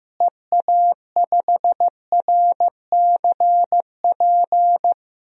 Morse code, heard as a keyed tone, about 15 wpm, 700 Hz. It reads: EA5RCP
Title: DXCC-100 Call Signs - 6 Characters - Repeated 3x - 15wpm